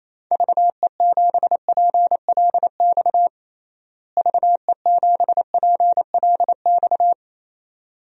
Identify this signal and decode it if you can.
Morse code, heard as a keyed tone, about 28 wpm, 700 Hz. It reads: VE7PLX VE7PLX
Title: DXCC-100 Call Signs - 6 Characters - Repeated 2x - 28wpm